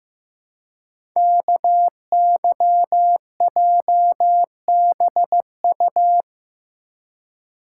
Morse code, heard as a keyed tone, about 15 wpm, 700 Hz. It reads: KYJBU
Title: Code Groups - 5 Letters - 15wpm